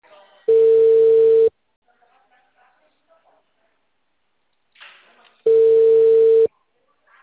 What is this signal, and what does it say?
Ringback tone (China, 450 Hz), with speech.